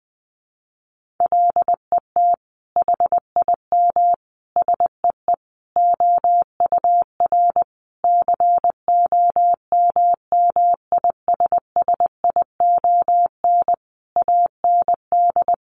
Morse code, heard as a keyed tone, about 20 wpm, 700 Hz. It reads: LET HIM SEE OUR COMMISSION AND
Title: Sentences from Top 1400 Words 20wpm